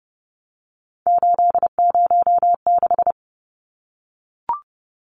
Morse code, heard as a keyed tone, about 30 wpm, 700 Hz. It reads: 806
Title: Numbers - 3-Digit 30wpm